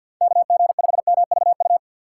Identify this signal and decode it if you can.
Morse code, heard as a keyed tone, about 53 wpm, 700 Hz. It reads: KG5MVU
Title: USA Call Signs - 6 Characters - 4x Character Spacing - 50wpm